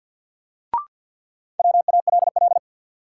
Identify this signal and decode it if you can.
Morse code, heard as a keyed tone, about 53 wpm, 700 Hz. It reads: WALL
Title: Sentences from Top 1100 Words - New Word Drill Alt - 50wpm